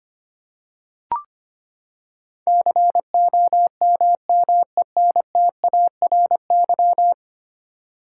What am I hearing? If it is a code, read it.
Morse code, heard as a keyed tone, about 25 wpm, 700 Hz. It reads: COMMENTARY
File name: Ten Letter Words - 25wpm